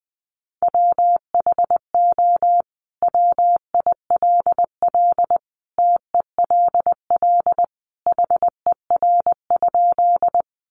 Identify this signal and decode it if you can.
Morse code, heard as a keyed tone, about 20 wpm, 700 Hz. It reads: WHO WILL TELL HER?
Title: Sentences from Top 200 Words 20wpm